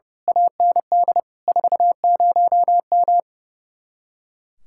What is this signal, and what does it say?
Morse code, heard as a keyed tone, about 28 wpm, 700 Hz. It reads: AND 40M
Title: Speed-Racing Common QSO Phrases Encore - 30wpm